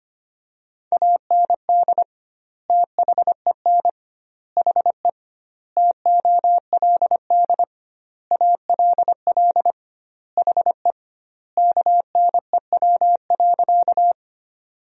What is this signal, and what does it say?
Morse code, heard as a keyed tone, about 25 wpm, 700 Hz. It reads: AND THEN HE TOLD ALL HE KNEW.
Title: 2x Word-Spacing - Sentences from Top 400 Words - 25wpm